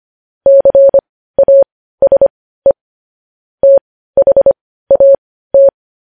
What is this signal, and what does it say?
Morse code, heard as a keyed tone, about 25 wpm, spaced as for 15 wpm, 550 Hz. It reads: CASE THAT
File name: podcast / News headlines-1x-25x15